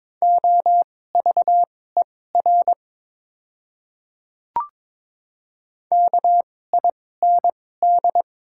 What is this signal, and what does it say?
Morse code, heard as a keyed tone, about 22 wpm, spaced as for 16 wpm, 700 Hz. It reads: OVER KIND
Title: 2x Character Spacing - Top 200 Words - 22wpm